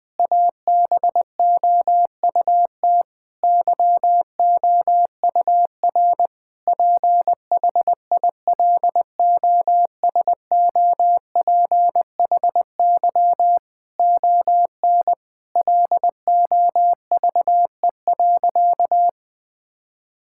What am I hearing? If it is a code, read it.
Morse code, heard as a keyed tone, about 20 wpm, 700 Hz. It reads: ABOUT YOUR PHILOSOPHY ON LOVE.
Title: Sentences from Top 1000 Words - Review Part 6 - 20wpm